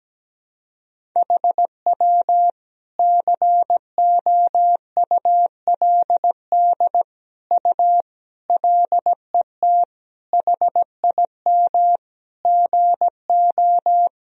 Morse code, heard as a keyed tone, about 17 wpm, 700 Hz. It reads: HW COULD U LET HIM GO
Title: Sentences from Top 300 Words and Abbreviations 17wpm